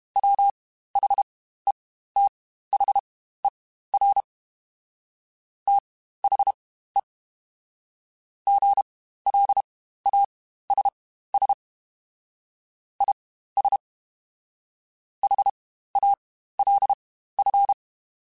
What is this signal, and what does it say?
Morse code, heard as a keyed tone, about 32 wpm, spaced as for 15 wpm, 800 Hz. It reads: WHETHER THE GLASS IS HALF
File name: Spaced / Chuckles Spaced-mp3s / HalfFull sp @30